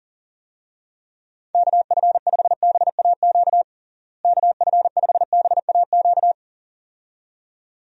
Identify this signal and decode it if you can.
Morse code, heard as a keyed tone, about 40 wpm, 700 Hz. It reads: KF5BAQ KF5BAQ
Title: USA Call Signs - 6 Characters - Repeated 2x - 40wpm